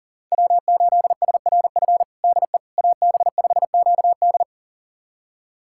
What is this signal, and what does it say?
Morse code, heard as a keyed tone, about 40 wpm, 700 Hz. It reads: W8SRF DE AB5QD
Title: USA Callsign DE Callsign - 5 Characters - 40wpm